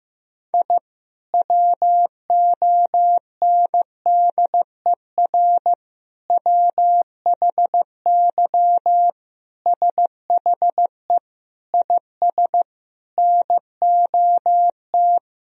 Morse code, heard as a keyed tone, about 15 wpm, 700 Hz. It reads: I WONDER WHY SHE IS NOT
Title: Sentences from Top 1000 Words - Review Part 3 - 15wpm